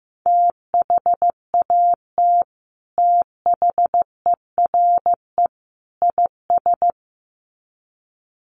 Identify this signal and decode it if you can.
Morse code, heard as a keyed tone, about 15 wpm, 700 Hz. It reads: THAT THERE IS
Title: Sets of 3 Words 15wpm